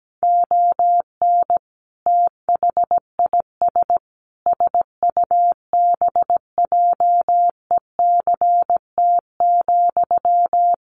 Morse code, heard as a keyed tone, about 17 wpm, 700 Hz. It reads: ON THIS SUBJECT,
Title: Sentences from Top 800 Words 17wpm